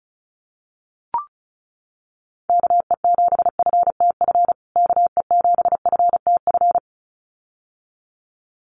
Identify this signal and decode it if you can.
Morse code, heard as a keyed tone, about 34 wpm, 700 Hz. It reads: KE7FTF KE7FTF
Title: USA Call Signs Repeated 2x 35wpm